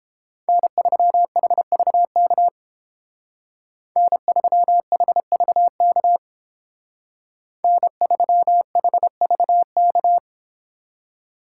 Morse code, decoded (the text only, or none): N3HVK N3HVK N3HVK